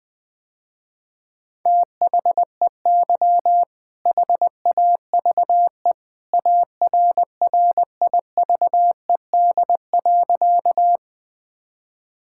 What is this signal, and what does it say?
Morse code, heard as a keyed tone, about 21 wpm, 700 Hz. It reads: THEY HAVE ARRIVED.
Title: Sentences with 2-Word Phrases - Spoken Beginning - 20wpm